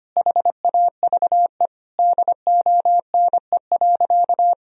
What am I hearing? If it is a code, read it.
Morse code, heard as a keyed tone, about 25 wpm, 700 Hz. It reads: HAVE DONE.
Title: Sentences from Top 1000 Words - Review Part 3 - 25wpm